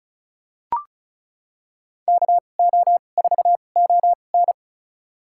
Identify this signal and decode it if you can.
Morse code, heard as a keyed tone, about 34 wpm, spaced as for 25 wpm, 700 Hz. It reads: KO4ON
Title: USA Call Signs - 5 Characters - 2x Character Spacing - 35wpm